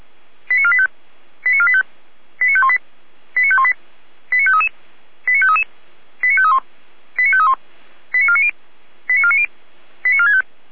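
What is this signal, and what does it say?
Five-tone selective call (ZVEI), heard as quick runs of five tones, tones of about 80 ms. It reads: 89485 89485 89518 89518 89533 89533 89531 89531 89490 89490 89465
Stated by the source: ZVEI five-tone call received by SDR.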